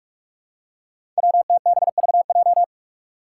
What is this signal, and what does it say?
Morse code, heard as a keyed tone, about 46 wpm, 700 Hz. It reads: WTBVJ
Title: Five Challenging Letter-Number Combinations 45wpm